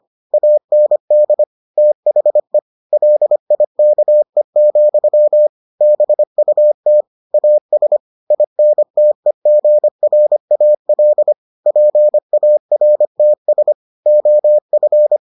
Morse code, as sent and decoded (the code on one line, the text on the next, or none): .- -. -.. / - .... / .-.. .. -.- . --..-- / -... ..- - / .- ... / .. -. - . --. .-. .- .-.. / .--. .- .-. - ... / --- ..-.
AND THE LIKE, BUT AS INTEGRAL PARTS OF